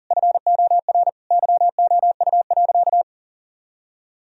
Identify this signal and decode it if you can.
Morse code, heard as a keyed tone, about 40 wpm, 700 Hz. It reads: FOR YOU.
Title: Sentences from Top 1000 Words - Review Part 4 - 40wpm